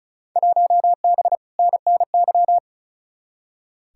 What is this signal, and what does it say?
Morse code, heard as a keyed tone, about 36 wpm, 700 Hz. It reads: 1B NNY
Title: Field Day Exchanges - 35wpm